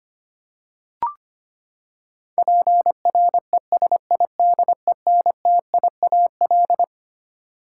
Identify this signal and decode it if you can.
Morse code, heard as a keyed tone, about 25 wpm, 700 Hz. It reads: PRESIDENTIAL
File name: Twelve Letter Words - 25wpm